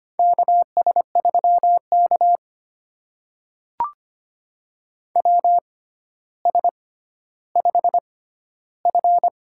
Morse code, heard as a keyed tone, about 25 wpm, spaced as for 13 wpm, 700 Hz. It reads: KS3K WS5F
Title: USA Call Signs - 4 Characters - 6x Character Spacing - 25wpm